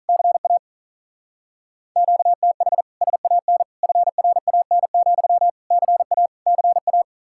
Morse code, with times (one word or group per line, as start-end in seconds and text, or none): 0.1-0.6 CA
2.0-2.8 QTH
3.0-3.6 SAN
3.8-5.5 FRAN,
5.7-6.3 CA
6.5-7.0 CA